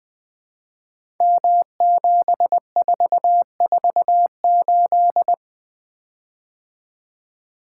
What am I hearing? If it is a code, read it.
Morse code, heard as a keyed tone, about 20 wpm, 700 Hz. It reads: M7448
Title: Code Groups - 5 Letter:Numbers - 20wpm